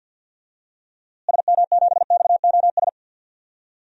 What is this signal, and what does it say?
Morse code, heard as a keyed tone, about 53 wpm, 700 Hz. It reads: SM7XOS
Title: DXCC-100 Call Signs - 6 Characters - Repeated 2x - 50wpm